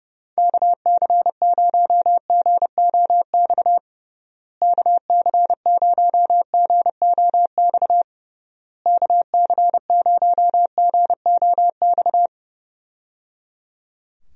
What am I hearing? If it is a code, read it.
Morse code, heard as a keyed tone, about 30 wpm, 700 Hz. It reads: KC0GOX KC0GOX KC0GOX
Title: USA Call Signs - 6 Characters - Repeated 3x - 30wpm